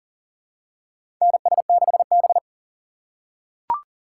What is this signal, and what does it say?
Morse code, heard as a keyed tone, about 40 wpm, 700 Hz. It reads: NS6B
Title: USA Call Signs - 4 Characters - 2x Character Spacing - 40wpm